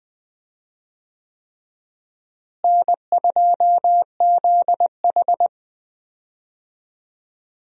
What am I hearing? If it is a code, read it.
Morse code, heard as a keyed tone, about 20 wpm, 700 Hz. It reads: N2ZH
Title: USA Call Signs - 4 Characters - Repeated 3x - 20wpm